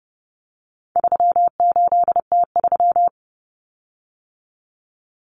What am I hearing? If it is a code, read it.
Morse code, heard as a keyed tone, about 30 wpm, 700 Hz. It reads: 38T3
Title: Common Cut Numbers - 4-Digit 30wpm